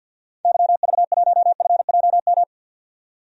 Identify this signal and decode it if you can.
Morse code, heard as a keyed tone, about 50 wpm, 700 Hz. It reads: YV1FJM
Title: DXCC-100 Call Signs - 6 Characters - 50wpm